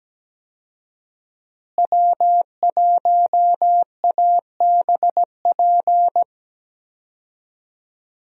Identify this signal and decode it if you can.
Morse code, heard as a keyed tone, about 17 wpm, 700 Hz. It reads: W1ABP
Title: USA Call Signs - 5 Characters - Repeated 2x - 17wpm